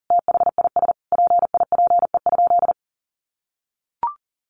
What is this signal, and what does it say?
Morse code, heard as a keyed tone, about 40 wpm, 700 Hz. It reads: THIS PIPE?
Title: Sentences from Top 1500 Words 40wpm